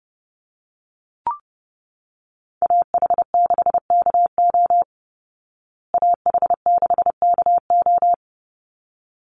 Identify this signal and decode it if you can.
Morse code, heard as a keyed tone, about 30 wpm, 700 Hz. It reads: AH6KO AH6KO
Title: Call Signs Repeated 3x - 30wpm